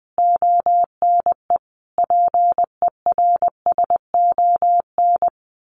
Morse code, heard as a keyed tone, about 20 wpm, 700 Hz. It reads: ONE PERSON
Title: Sentences from Top 1400 Words 20wpm